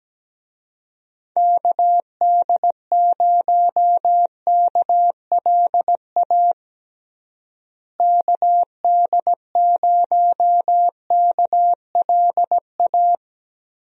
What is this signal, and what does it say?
Morse code, heard as a keyed tone, about 17 wpm, 700 Hz. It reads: KD0KLA KD0KLA
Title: USA Call Signs - 6 Characters - Repeated 3x - 17wpm